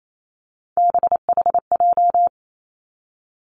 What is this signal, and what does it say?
Morse code, heard as a keyed tone, about 28 wpm, 700 Hz. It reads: BHJ
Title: Three Challenging Letter-Number Combinations 28wpm